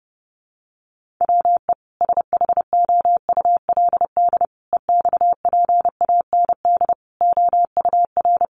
Morse code, heard as a keyed tone, about 30 wpm, 700 Hz. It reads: WE SHOULD EXPAND OUR
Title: Sentences from Top 1400 Words 30wpm